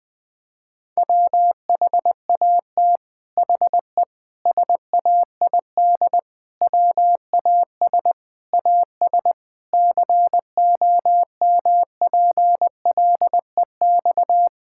Morse code, heard as a keyed tone, about 20 wpm, 700 Hz. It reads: WHAT HE SAID WAS AS COMPLEX
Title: Sentences from Top 1000 Words - Review Part 6 - 20wpm